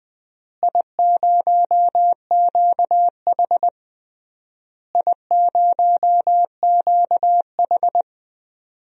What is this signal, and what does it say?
Morse code, heard as a keyed tone, about 20 wpm, 700 Hz. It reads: I0QH I0QH
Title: DXCC-100 Call Signs - 4 Characters - Repeated 3x - 20wpm